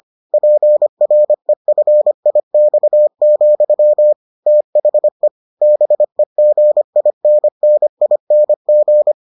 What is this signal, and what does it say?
Morse code, heard as a keyed tone, about 25 wpm, 600 Hz. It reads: PREFIX, THE BEGINNING